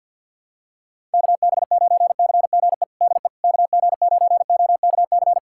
Morse code, heard as a keyed tone, about 50 wpm, 700 Hz. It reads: KB9CGE DE KG9OKC
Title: USA Callsign DE Callsign - 6 Characters - 50wpm